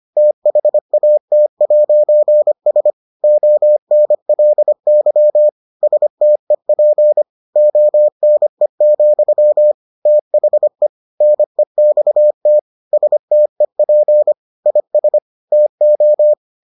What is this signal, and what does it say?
Morse code, heard as a keyed tone, about 25 wpm, 600 Hz. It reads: THAT'S ONLY STEP ONE, THE NEXT STEP IS TO